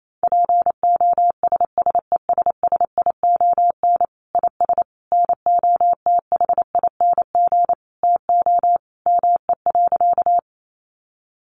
Morse code, heard as a keyed tone, about 28 wpm, 700 Hz. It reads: POSSESSION IS NOTHING TO ME.
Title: Sentences from Top 1400 Words 28wpm